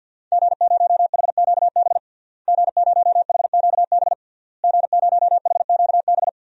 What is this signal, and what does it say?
Morse code, heard as a keyed tone, about 49 wpm, 700 Hz. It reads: G0HQB G0HQB G0HQB
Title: DXCC-100 Call Signs - 5 Characters - Repeated 3x - 50wpm